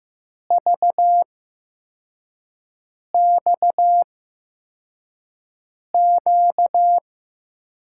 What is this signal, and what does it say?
Morse code, heard as a keyed tone, about 15 wpm, spaced as for 4 wpm, 700 Hz. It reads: VXQ
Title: USA Call Signs - 5 Characters - 8x Character Spacing - 15wpm